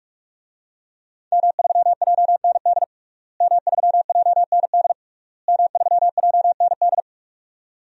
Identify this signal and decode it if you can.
Morse code, heard as a keyed tone, about 48 wpm, 700 Hz. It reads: M3JND M3JND M3JND
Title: DXCC-100 Call Signs - 5 Characters - Repeated 3x - 45wpm